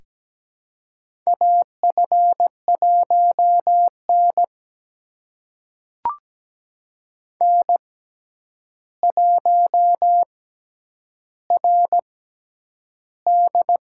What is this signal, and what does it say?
Morse code, heard as a keyed tone, about 17 wpm, spaced as for 9 wpm, 700 Hz. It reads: AF1N N1RD
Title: USA Call Signs - 4 Characters - 6x Character Spacing - 17wpm